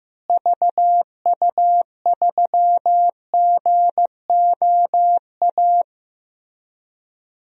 Morse code, heard as a keyed tone, about 16 wpm, 700 Hz. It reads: VU3GOA
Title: DXCC-100 Call Signs - 6 Characters - Repeated 2x - 15wpm